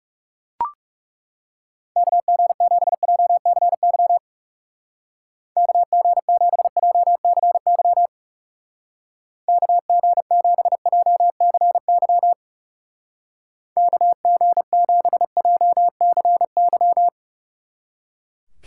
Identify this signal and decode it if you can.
Morse code, heard as a keyed tone, about 37 wpm, 700 Hz. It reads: KG7JCY KG7JCY KG7JCY KG7JCY
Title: Speed Racing - USA Call Signs 30wpm